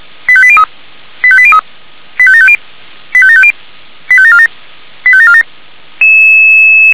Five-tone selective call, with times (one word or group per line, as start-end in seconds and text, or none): 0.3-0.6 85803
1.2-1.6 85803
2.2-2.6 85750
3.1-3.5 85750
4.1-4.5 85747
5.1-5.4 85747